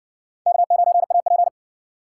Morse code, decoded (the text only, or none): K9AP